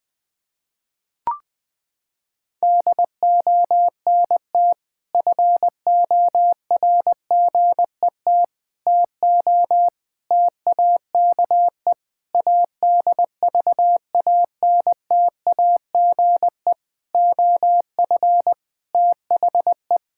Morse code, heard as a keyed tone, about 20 wpm, 700 Hz. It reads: DONT FORGET TO TAKE ADVANTAGE OF THE
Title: Sentences with 5-Word Phrases - 20wpm